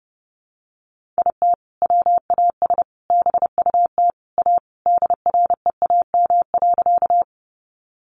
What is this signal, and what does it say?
Morse code, heard as a keyed tone, about 28 wpm, 700 Hz. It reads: IT WAS BUT A DREAM.
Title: Sentences from Top 1500 Words 30wpm